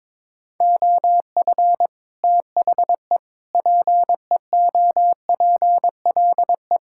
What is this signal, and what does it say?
Morse code, heard as a keyed tone, about 22 wpm, 700 Hz. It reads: OF THE PEOPLE
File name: practice / 22wpm / Sentences with 8-Word Phrases - 22wpm